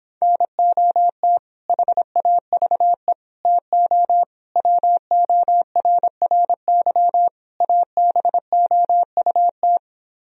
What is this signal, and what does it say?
Morse code, heard as a keyed tone, about 26 wpm, 700 Hz. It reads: NOT HAVE TO WORRY ABOUT